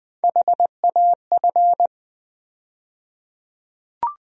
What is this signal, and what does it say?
Morse code, heard as a keyed tone, about 20 wpm, 700 Hz. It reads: HAF
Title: USA Call Sign Postfixes - 3 Letters 20wpm